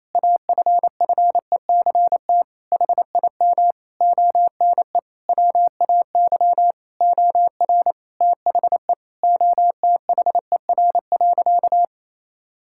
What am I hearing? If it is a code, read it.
Morse code, heard as a keyed tone, about 28 wpm, 700 Hz. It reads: AFFECT HIM ONE WAY OR THE OTHER.
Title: Sentences from Top 1000 Words - Review Part 6 - 28wpm